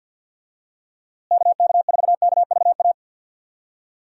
Morse code, heard as a keyed tone, about 49 wpm, 700 Hz. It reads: KK4KVA
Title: USA Call Signs - 6 Characters - Repeated 2x - 50wpm